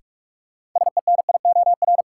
Morse code, heard as a keyed tone, about 46 wpm, 700 Hz. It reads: SENIOR
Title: Sets of 5 Words Encore - New Word Drill - 45wpm